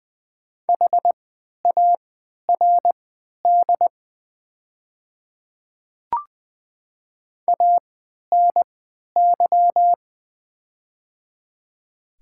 Morse code, heard as a keyed tone, about 20 wpm, spaced as for 11 wpm, 690 Hz. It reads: HARD ANY
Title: 3x Character Spacing - Top 300 Words - 20wpm